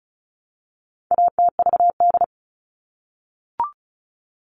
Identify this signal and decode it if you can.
Morse code, heard as a keyed tone, about 34 wpm, 700 Hz. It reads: ATVD
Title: All Cut Numbers - 4-Digit 35wpm